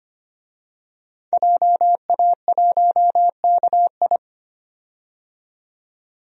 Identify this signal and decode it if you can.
Morse code, heard as a keyed tone, about 25 wpm, 700 Hz. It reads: JA1KI
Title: DXCC-100 Call Signs - 5 Characters - 25wpm